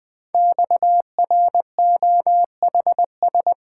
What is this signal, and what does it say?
Morse code, heard as a keyed tone, about 20 wpm, 700 Hz. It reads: XROHS
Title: Code Groups - 5 Letters - 20wpm